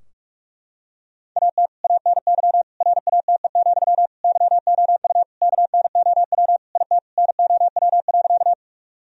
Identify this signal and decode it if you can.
Morse code, heard as a keyed tone, about 41 wpm, 700 Hz. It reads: AT ANY RATE, YOU KNOW IT NOW.
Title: Sentences from Top 1000 Words - Review Part 4 - 45wpm